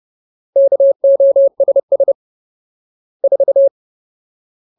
Morse code, heard as a keyed tone, about 30 wpm, 550 Hz. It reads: KOSS 4
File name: Ultimate Word List / 30WPM Ultimate